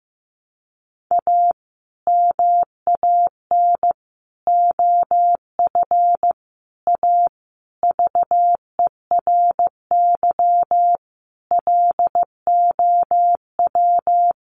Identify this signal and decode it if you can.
Morse code, heard as a keyed tone, about 15 wpm, 700 Hz. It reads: A MAN OF A VERY LOW
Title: Sentences from Top 1200 Words 15wpm